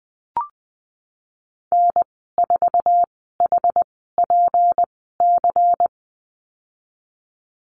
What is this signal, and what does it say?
Morse code, heard as a keyed tone, about 20 wpm, spaced as for 15 wpm, 700 Hz. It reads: N4HPC
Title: 167 Lesson 27g - Number 7 - Call Signs 20wpm